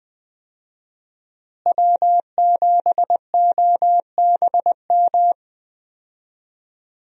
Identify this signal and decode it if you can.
Morse code, heard as a keyed tone, about 20 wpm, 700 Hz. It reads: W7OBM
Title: USA Call Signs - 5 Characters - Repeated 2x - 20wpm